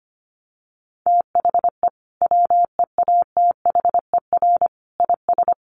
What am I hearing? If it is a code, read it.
Morse code, heard as a keyed tone, about 25 wpm, 700 Hz. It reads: THE WEATHER IS
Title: Sentences from Top 1200 Words 25wpm